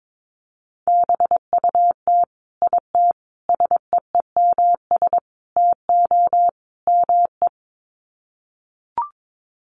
Morse code, heard as a keyed tone, about 22 wpm, 700 Hz. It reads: BUT IT SEEMS TO ME
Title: Sets of 5 Words Encore2 22wpm